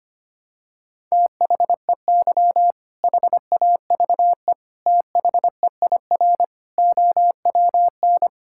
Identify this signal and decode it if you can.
Morse code, heard as a keyed tone, about 25 wpm, 700 Hz. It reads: THEY HAVE THEIR OWN
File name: practice / 25wpm / Sentences with 2-Word Phrases - 25wpm